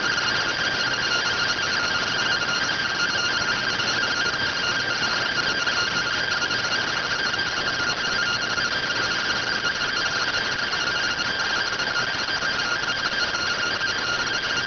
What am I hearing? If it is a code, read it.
Two-tone teleprinter signal, about 170 Hz shift.